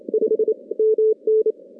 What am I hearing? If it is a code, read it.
Morse code, heard as a keyed tone, about 28 wpm, 410 Hz. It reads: 5 MN
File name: contests / CW-SS-k5tr- -2200